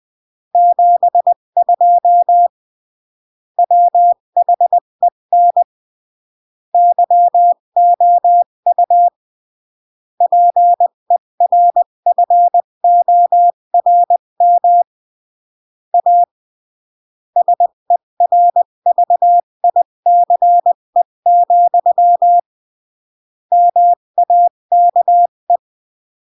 Morse code, heard as a keyed tone, about 20 wpm, 700 Hz. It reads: 72 WHEN YOU PERFORM A SERVICE, MAKE
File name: CW / jodorowsky / 82 commandments 20-15